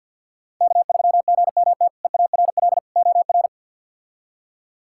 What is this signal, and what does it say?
Morse code, heard as a keyed tone, about 50 wpm, 700 Hz. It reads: K3GMT EARL OR
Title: K1USN SST QSO Practice - 50wpm